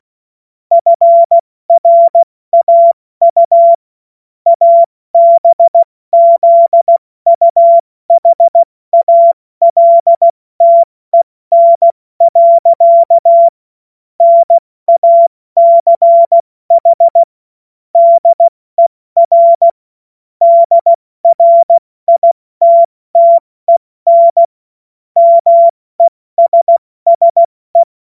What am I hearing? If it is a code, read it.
Morse code, heard as a keyed tone, about 16 wpm, 680 Hz. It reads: FRAU ABZUHALTEN. NACH DER DRITTEN MESSE
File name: Morsekurs / Witze / Witz 09. Dezember 2025 70 F80